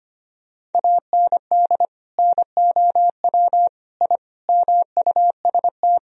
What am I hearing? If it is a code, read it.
Morse code, heard as a keyed tone, about 25 wpm, 700 Hz. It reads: AND NOW I MUST
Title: Sentences from Top 1000 Words - Review Part 3 - 25wpm